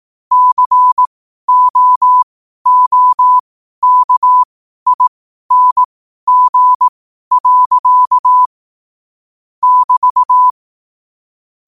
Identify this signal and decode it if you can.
Morse code, heard as a keyed tone, about 18 wpm, spaced as for 12 wpm, 1000 Hz. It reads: COOKING. <BT>
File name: qotdfiles / 12wpm